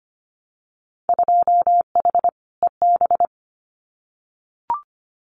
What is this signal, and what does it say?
Morse code, heard as a keyed tone, about 25 wpm, 700 Hz. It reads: 2H EB
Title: Winter Field Day Exchanges Encore 25wpm